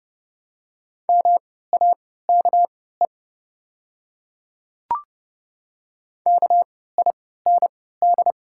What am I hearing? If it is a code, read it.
Morse code, heard as a keyed tone, about 29 wpm, spaced as for 17 wpm, 700 Hz. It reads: MAKE KIND
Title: 3x Character Spacing - Top 200 Words - 30wpm